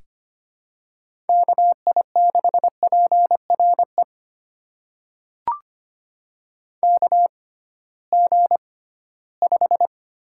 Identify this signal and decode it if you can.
Morse code, heard as a keyed tone, about 25 wpm, spaced as for 16 wpm, 700 Hz. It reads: KI6PRE KG5
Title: USA Call Signs - 6 Characters - 6x Character Spacing - 25wpm